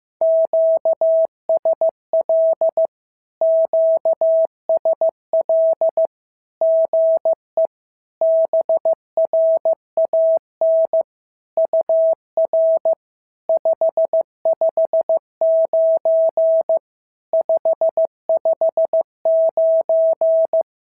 Morse code, heard as a keyed tone, about 15 wpm, 650 Hz. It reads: QSL QSL GE BRAN UR 559 559
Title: SOTA QSO Practice 15wpm